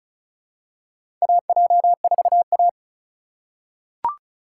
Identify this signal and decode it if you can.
Morse code, heard as a keyed tone, about 36 wpm, 700 Hz. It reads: AJ4A
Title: USA Call Signs - 4x Character Spacing - 35wpm